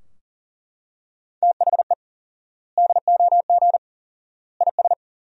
Morse code, heard as a keyed tone, about 40 wpm, 700 Hz. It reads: THE DOG IS
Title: 4x Word-Spacing - Sentences from Top 400 Words - 40wpm